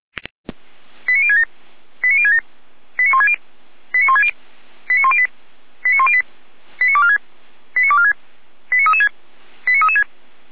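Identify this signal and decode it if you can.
Five-tone selective call (ZVEI), heard as quick runs of five tones, tones of about 70 ms. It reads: 89067 89067 89160 89160 89108 89108 89246 89246 89336 89336
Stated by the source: ZVEI five-tone call 89072 received by SDR.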